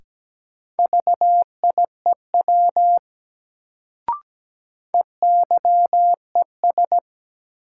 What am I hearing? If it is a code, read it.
Morse code, heard as a keyed tone, about 17 wpm, 700 Hz. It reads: VIEW EYES
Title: Sets of 5 Words - New Word Drill - 17wpm